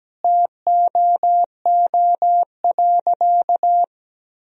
Morse code, heard as a keyed tone, about 17 wpm, 700 Hz. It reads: TOO.
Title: Sentences from Top 1000 Words - Review Part 1 - 17wpm